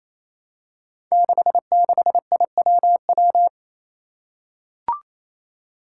Morse code, heard as a keyed tone, about 28 wpm, 700 Hz. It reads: 66IWW
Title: Five Challenging Letter-Number Combinations 28wpm